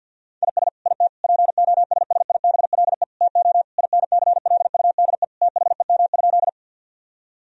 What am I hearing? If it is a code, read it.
Morse code, heard as a keyed tone, about 50 wpm, 700 Hz. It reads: IS IT POSSIBLE TO INCLUDE THEM?
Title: Sentences from Top 1000 Words - Review Part 3 - 50wpm